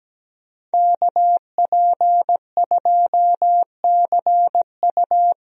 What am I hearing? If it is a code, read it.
Morse code, heard as a keyed tone, about 17 wpm, 700 Hz. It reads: KP2CU
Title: DXCC-100 Call Signs - 5 Characters - 17wpm